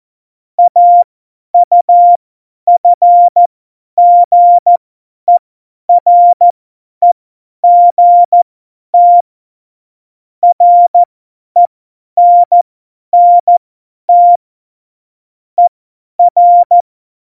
Morse code, heard as a keyed tone, about 14 wpm, spaced as for 10 wpm, 700 Hz. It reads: AUFGEREGT RENNT ER
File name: Morsekurs / Witze / Witz 24. Oktober 17, Tempo 50